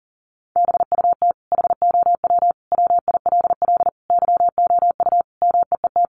Morse code, heard as a keyed tone, about 40 wpm, 700 Hz. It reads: BUT HOW WILL YOU MEET
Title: Sentences from Top 1100 Words 40wpm